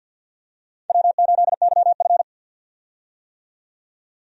Morse code, heard as a keyed tone, about 53 wpm, 700 Hz. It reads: W8YF
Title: USA Call Signs - 4 Characters - 50wpm